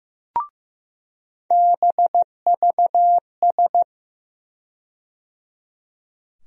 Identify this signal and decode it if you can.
Morse code, heard as a keyed tone, about 15 wpm, 700 Hz. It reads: BVS
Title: USA Call Sign Postfixes - 3 Letters 15wpm